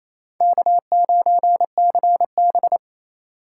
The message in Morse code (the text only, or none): K9CB